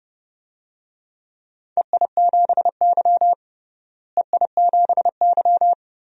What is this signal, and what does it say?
Morse code, heard as a keyed tone, about 30 wpm, 700 Hz. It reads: EI7Y EI7Y
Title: DXCC-100 Call Signs - 4 Characters - Repeated 2x - 30wpm